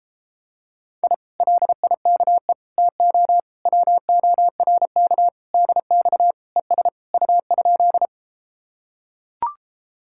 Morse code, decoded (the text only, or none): I LIKE TO WORK DX ES U?